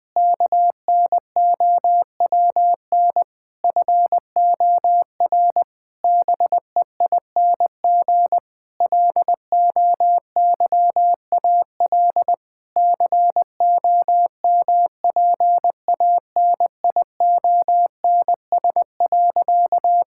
Morse code, heard as a keyed tone, about 19 wpm, 700 Hz. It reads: KNOWN FOR BEING LOYAL COMPANIONS.
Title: Difficult Sentences with Hints - 20wpm